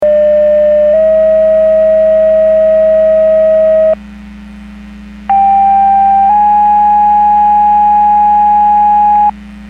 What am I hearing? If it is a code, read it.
Two-tone page, heard as a one-second tone, then a three-second tone, about 600 then 640 Hz, about 790 then 830 Hz. It is heard from 0.0 to 9.3 s.